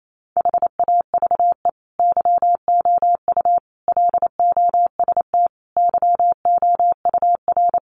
Morse code, heard as a keyed tone, about 28 wpm, 700 Hz. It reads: HAVE YOU LOST YOUR
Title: Sentences from Top 1400 Words 28wpm